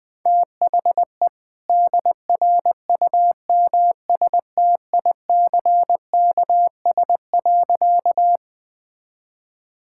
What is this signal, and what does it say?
Morse code, heard as a keyed tone, about 21 wpm, 700 Hz. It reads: THE DRUMSTICKS.
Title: All-Age Appropriate Jokes - 20wpm